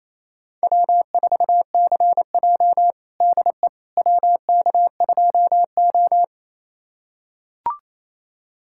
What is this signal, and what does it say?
Morse code, heard as a keyed tone, about 28 wpm, 700 Hz. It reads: W4CJ DE WK2O
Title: USA Callsign DE Callsign - 4 Characters - 28wpm